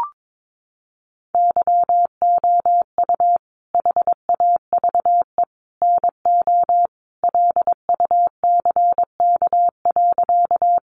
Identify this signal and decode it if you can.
Morse code, heard as a keyed tone, about 22 wpm, 700 Hz. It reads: YOU HAVE NO LUCK.
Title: Sentences from Top 1500 Words 22wpm